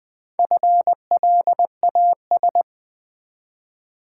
Morse code, heard as a keyed tone, about 20 wpm, 700 Hz. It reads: FLAS
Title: Build A Word - 1 Letter At A Time - 5-Letter Words - 20wpm